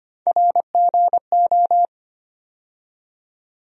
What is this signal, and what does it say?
Morse code, heard as a keyed tone, about 25 wpm, 700 Hz. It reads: RGO
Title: USA Call Sign Postfixes - 3 Letters 25wpm